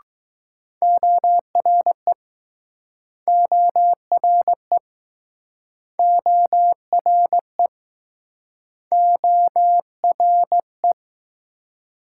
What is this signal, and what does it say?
Morse code, heard as a keyed tone, about 19 wpm, 700 Hz. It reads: ORE ORE ORE ORE